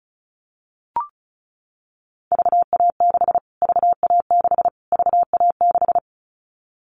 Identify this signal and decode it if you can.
Morse code, heard as a keyed tone, about 34 wpm, 700 Hz. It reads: VA6 VA6 VA6
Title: Callsign Prefixes for Top 25 DXCC - Repeated 3x - 35wpm